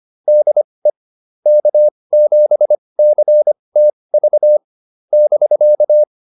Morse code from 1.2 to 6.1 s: K7CTV <BK>

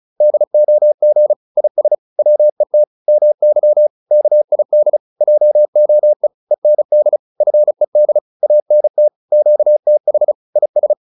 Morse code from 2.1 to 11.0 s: WET MY KID JOE END FED ANT QTH IS

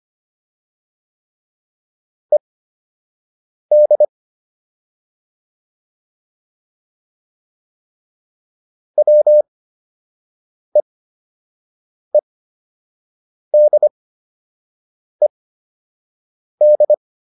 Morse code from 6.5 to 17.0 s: WEEDED

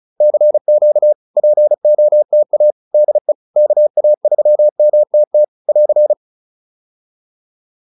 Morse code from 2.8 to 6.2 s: DE KA3MTT <AR>

